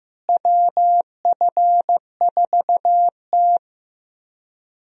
Morse code, heard as a keyed tone, about 16 wpm, 700 Hz. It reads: WF4T
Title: USA Call Signs - 4 Characters - Repeated 3x - 15wpm